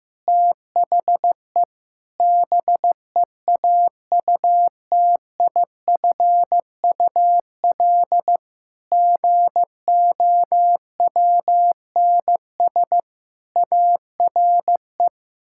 Morse code, heard as a keyed tone, about 15 wpm, 700 Hz. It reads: THE BEAUTIFUL GOWNS ARE
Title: Pangram Challenge v2 - 15wpm